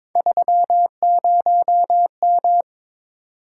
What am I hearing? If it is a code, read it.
Morse code, heard as a keyed tone, about 22 wpm, 700 Hz. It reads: 30M
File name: practice / 22wpm / Three Letter-Number Combinations - 22wpm